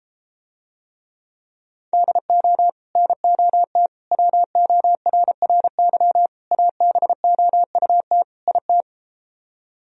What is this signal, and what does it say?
Morse code, heard as a keyed tone, about 33 wpm, 700 Hz. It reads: DO NOT WORRY ABOUT IT